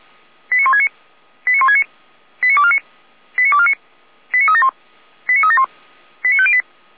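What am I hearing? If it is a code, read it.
Five-tone selective call (ZVEI), heard as quick runs of five tones, tones of about 70 ms. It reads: 89169 89169 89249 89249 89371 89371 89508
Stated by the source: ZVEI five-tone call received by SDR, from the feuerwehr-orlamuende.org server.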